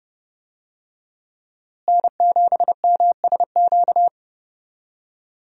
Morse code, heard as a keyed tone, about 32 wpm, 700 Hz. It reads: N7MSQ
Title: USA Call Signs - 5 Characters - Repeated 3x - 30wpm